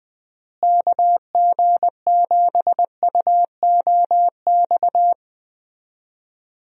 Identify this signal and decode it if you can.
Morse code, heard as a keyed tone, about 20 wpm, 700 Hz. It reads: KG7UOX
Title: USA Call Signs - 6 Characters - Repeated 2x - 20wpm